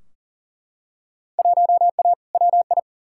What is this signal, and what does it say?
Morse code, heard as a keyed tone, about 40 wpm, 700 Hz. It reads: 1A WI
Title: Field Day Exchanges - 40wpm